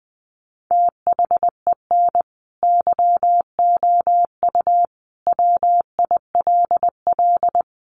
Morse code, heard as a keyed tone, about 20 wpm, 700 Hz. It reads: THEN YOU WILL
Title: Sentences from Top 400 Words 20wpm